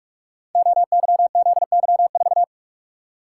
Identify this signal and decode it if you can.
Morse code, heard as a keyed tone, about 44 wpm, 700 Hz. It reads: OYZY4
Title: Code Groups - 5 Letter:Numbers - 45wpm